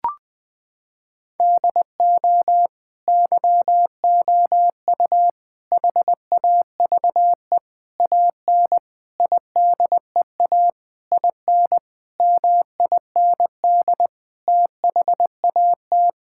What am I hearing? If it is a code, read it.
Morse code, heard as a keyed tone, about 21 wpm, 700 Hz. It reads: DO YOU HAVE AN IDEA IN MIND THAT
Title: Sets of 9 Words - 20wpm